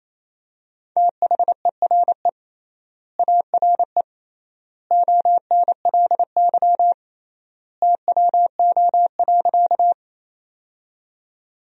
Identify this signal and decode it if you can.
Morse code, heard as a keyed tone, about 27 wpm, 700 Hz. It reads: THERE ARE ONLY TWO.
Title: 3x Word-Spacing - Sentences from Top 200 Words - 28wpm